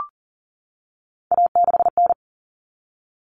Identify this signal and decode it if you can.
Morse code, heard as a keyed tone, about 40 wpm, 700 Hz. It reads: A6N
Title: All Cut Numbers - 3-Digit 40wpm